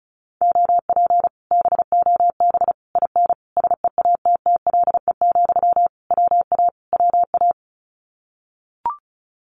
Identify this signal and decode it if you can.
Morse code, heard as a keyed tone, about 36 wpm, 700 Hz. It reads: OP BOB IN SEATTLE, WA WA
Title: Common QSO Phrases Encore 35wpm